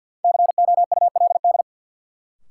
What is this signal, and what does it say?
Morse code, heard as a keyed tone, about 52 wpm, 700 Hz. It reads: COULD
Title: Sets of 2 Words - New Word Drill - 50wpm